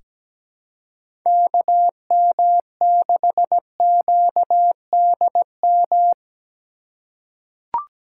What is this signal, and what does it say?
Morse code, heard as a keyed tone, about 17 wpm, 700 Hz. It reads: KM6QDM
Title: USA Call Signs - 4x Character Spacing - 17wpm